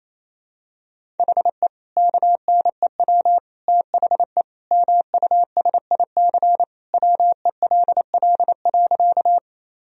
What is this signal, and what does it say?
Morse code, heard as a keyed tone, about 28 wpm, 700 Hz. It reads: HE KNEW THE MUSIC WELL.
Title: Sentences from Top 1000 Words - Review Part 1 - 28wpm